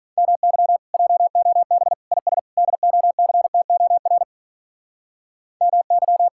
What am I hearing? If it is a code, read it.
Morse code, heard as a keyed tone, about 44 wpm, 700 Hz. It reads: MY JOB IS DOCTOR MY